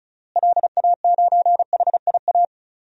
Morse code, heard as a keyed tone, about 34 wpm, 700 Hz. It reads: LA9HIA
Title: DXCC-100 Call Signs - 6 Characters - Repeated 3x - 35wpm